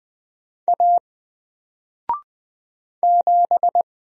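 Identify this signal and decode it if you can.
Morse code, heard as a keyed tone, about 20 wpm, 700 Hz. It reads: A 7
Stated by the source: Single letter and number Morse drill at 20 wpm, from N4LCD.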